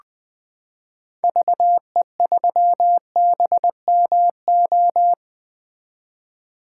Morse code, heard as a keyed tone, about 20 wpm, 700 Hz. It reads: VE3BMO
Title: 2023 POTA Activator Call Signs - 20wpm